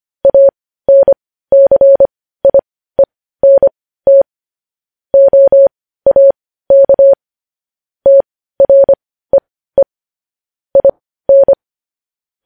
Morse code, heard as a keyed tone, about 25 wpm, spaced as for 15 wpm, 550 Hz. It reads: ANCIENT OAK TREE IN
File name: podcast / News headlines-1x-25x15